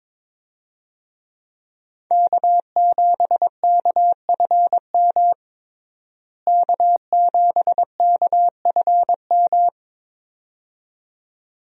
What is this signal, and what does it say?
Morse code, heard as a keyed tone, about 21 wpm, 700 Hz. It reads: K7KFM K7KFM
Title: USA Call Signs - 5 Characters - Repeated 2x - 22wpm